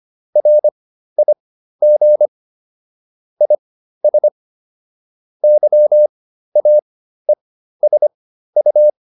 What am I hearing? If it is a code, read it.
Morse code, heard as a keyed tone, about 25 wpm, spaced as for 13 wpm, 600 Hz. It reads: RIG IS YAESU
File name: QSO409 13